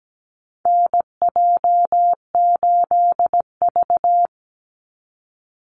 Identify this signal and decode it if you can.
Morse code, heard as a keyed tone, about 17 wpm, 700 Hz. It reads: NJ8V
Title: Call Signs - 17wpm